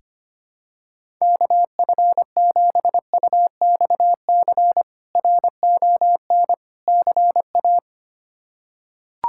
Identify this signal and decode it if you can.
Morse code, heard as a keyed tone, about 25 wpm, 700 Hz. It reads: KF7UXC RON CA
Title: K1USN SST QSO Practice - 25wpm